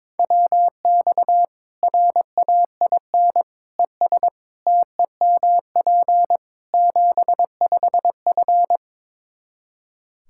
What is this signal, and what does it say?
Morse code, heard as a keyed tone, about 22 wpm, 700 Hz. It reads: WX RAIN ES TEMP 75F
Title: Practice QSO - Exchange 2 - Info - 22wpm